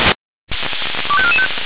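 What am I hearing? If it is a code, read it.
Five-tone selective call (ZVEI), heard as quick runs of five tones, tones of about 70 ms. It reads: 25445
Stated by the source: ZVEI five-tone selective call received by SDR.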